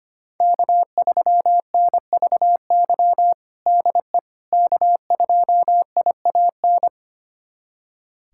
Morse code, heard as a keyed tone, about 25 wpm, 700 Hz. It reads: K3NVY DE K2IAN
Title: USA Callsign DE Callsign - 5 Characters - 25wpm